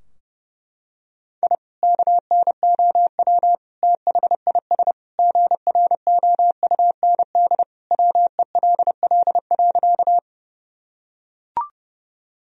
Morse code, heard as a keyed tone, about 30 wpm, 700 Hz. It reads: I KNOW THIS GROUND WELL.
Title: Sentences from Top 1000 Words - Review Part 2 - 30wpm